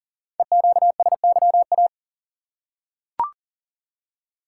Morse code, heard as a keyed tone, about 39 wpm, 690 Hz. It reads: EQSYA
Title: Code Groups - 5 Letters - 40wpm